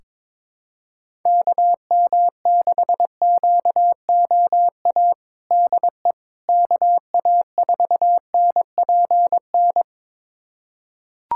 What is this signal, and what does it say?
Morse code, heard as a keyed tone, about 22 wpm, 700 Hz. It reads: KM6QOA DE KA4NPN
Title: USA Callsign DE Callsign - 6 Characters - 22wpm